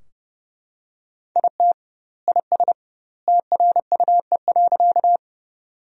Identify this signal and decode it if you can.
Morse code, heard as a keyed tone, about 30 wpm, 700 Hz. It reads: IT IS TRUE.
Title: 2x Word-Spacing - Sentences from Top 400 Words - 30wpm